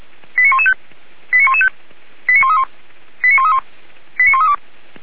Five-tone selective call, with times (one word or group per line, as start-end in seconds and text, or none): 0.4-0.7 89115
1.3-1.7 89115
2.3-2.6 89131
3.2-3.6 89131
4.2-4.5 89132